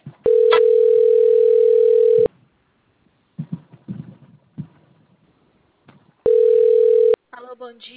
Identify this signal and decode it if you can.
Ringback tone (North America), 440 + 480 Hz.